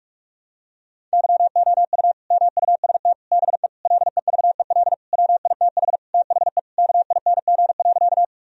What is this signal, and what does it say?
Morse code, heard as a keyed tone, about 41 wpm, 700 Hz. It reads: YOU MUST BE LEVEL WITH THE KING.
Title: Sentences from Top 1000 Words - Review Part 4 - 45wpm